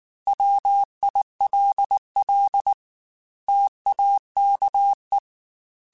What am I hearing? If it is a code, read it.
Morse code, heard as a keyed tone, about 19 wpm, 800 Hz. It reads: WILL TAKE
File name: Technical Subjects / Technical Subjects-mp3s / OR Gate @18